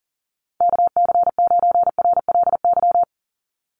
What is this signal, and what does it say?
Morse code, heard as a keyed tone, about 40 wpm, 700 Hz. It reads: KC9RLY
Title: USA Call Signs 40wpm